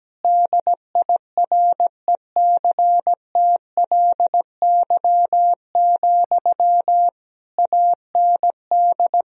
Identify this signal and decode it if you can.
Morse code, heard as a keyed tone, about 17 wpm, 700 Hz. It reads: DIRECTLY, AND
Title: Sentences from Top 1000 Words - Review Part 1 - 17wpm